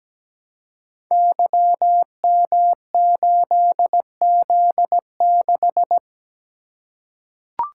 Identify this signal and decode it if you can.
Morse code, heard as a keyed tone, about 17 wpm, 700 Hz. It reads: YM8Z6
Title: Code Groups - 5 Letter:Numbers - 17wpm